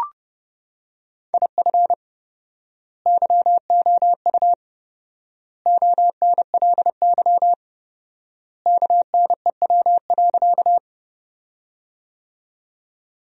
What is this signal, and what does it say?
Morse code, heard as a keyed tone, about 32 wpm, 700 Hz. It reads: IF YOU ONLY KNEW.
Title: 4x Word-Spacing - Sentences from Top 400 Words - 30wpm